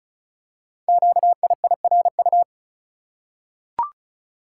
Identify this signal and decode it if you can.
Morse code, heard as a keyed tone, about 36 wpm, 700 Hz. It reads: QIIRU
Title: Code Groups - 5 Letters - 35wpm